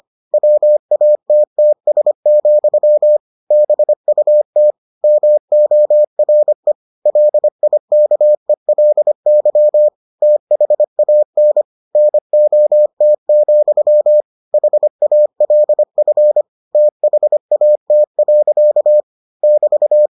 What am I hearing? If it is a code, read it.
Morse code, heard as a keyed tone, about 25 wpm, 600 Hz. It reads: WATTS, BUT MORE LIKELY THAN NOT, HALF THAT. <BT>